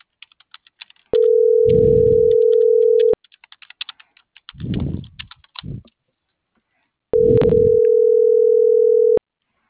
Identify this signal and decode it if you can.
Ringback tone (North America), 440 + 480 Hz.